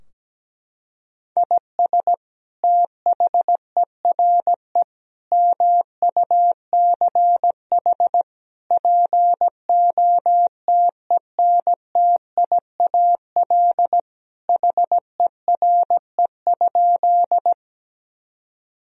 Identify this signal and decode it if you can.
Morse code, heard as a keyed tone, about 17 wpm, 700 Hz. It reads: IS THERE MUCH POTENTIAL HERE?
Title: Sentences from Top 1000 Words - Review Part 4 - 17wpm